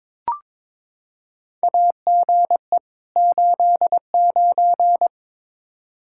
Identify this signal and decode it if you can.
Morse code, heard as a keyed tone, about 22 wpm, 700 Hz. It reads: AGE 89
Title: K1USN SST QSO Practice 22wpm